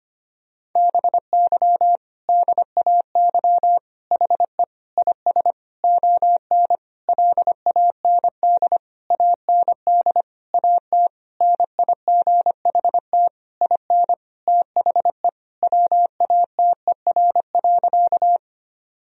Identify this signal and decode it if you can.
Morse code, heard as a keyed tone, about 25 wpm, 700 Hz. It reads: BY DAY HE IS ON LAND AND AT NIGHT IN THE WATER.
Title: Sentences from Top 1000 Words - Review Part 1 - 25wpm